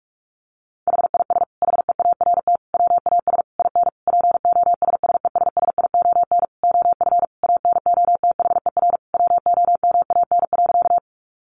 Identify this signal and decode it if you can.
Morse code, heard as a keyed tone, about 44 wpm, 700 Hz. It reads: HIS HEART WAS IN POSSESSION OF ANOTHER WOMAN.
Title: Sentences from Top 1400 Words 45wpm